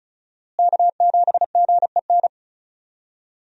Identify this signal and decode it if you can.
Morse code, heard as a keyed tone, about 36 wpm, 700 Hz. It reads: K7GEN
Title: USA Call Signs - 5 Characters - Repeated 3x - 35wpm